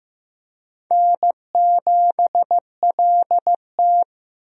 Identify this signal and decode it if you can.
Morse code, heard as a keyed tone, about 15 wpm, 700 Hz. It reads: N7LT
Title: USA Call Signs - 4 Characters - Repeated 2x - 15wpm